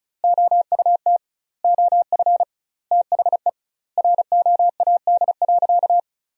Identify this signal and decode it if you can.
Morse code, heard as a keyed tone, about 36 wpm, 700 Hz. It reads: OUT OF THE ROAD.
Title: 2x Word-Spacing - Sentences from Top 400 Words - 35wpm